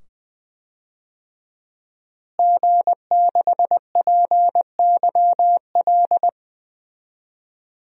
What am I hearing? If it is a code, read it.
Morse code, heard as a keyed tone, about 20 wpm, 700 Hz. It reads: G6PYL
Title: DXCC-100 Call Signs - 5 Characters - Repeated 2x - 20wpm